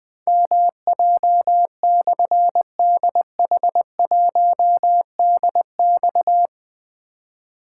Morse code, heard as a keyed tone, about 20 wpm, 700 Hz. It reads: MJ/DH1DX
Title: Difficult Call Signs 20wpm